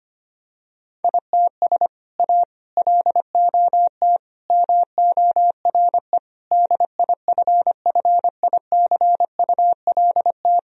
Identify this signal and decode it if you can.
Morse code, heard as a keyed tone, about 25 wpm, 700 Hz. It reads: ITS A LOT MORE DIFFICULT
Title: Sentences from Top 1000 Words - Review Part 3 - 25wpm